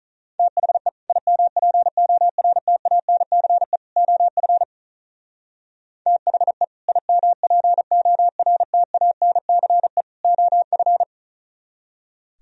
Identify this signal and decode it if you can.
Morse code, heard as a keyed tone, about 45 wpm, 700 Hz. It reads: THE IMPORTANCE OF THE IMPORTANCE OF